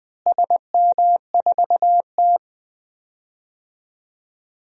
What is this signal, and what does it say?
Morse code, heard as a keyed tone, about 20 wpm, 700 Hz. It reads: SM4T
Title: DXCC-100 Call Signs - 4 Characters - Repeated 3x - 20wpm